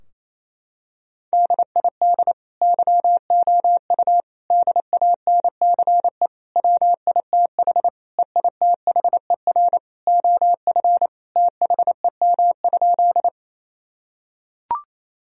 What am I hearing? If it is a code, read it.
Morse code, heard as a keyed tone, about 28 wpm, 700 Hz. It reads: DID YOU DANCE WITH EITHER OF THEM?
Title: Sentences from Top 1000 Words - Review Part 4 - 28wpm